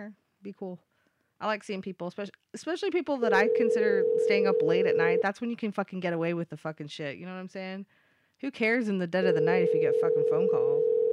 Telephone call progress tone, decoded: ringback tone